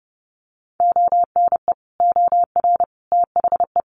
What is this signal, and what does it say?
Morse code, heard as a keyed tone, about 30 wpm, 700 Hz. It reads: ONE OR THE
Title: Sentences from Top 400 Words 30wpm